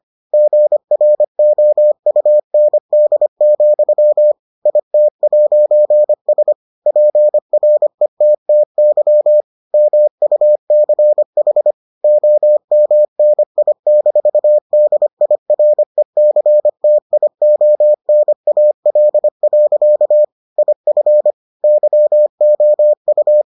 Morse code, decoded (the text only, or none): GROUND, IT'S PRETTY MUCH OMNI-DIRECTIONAL. IF YOU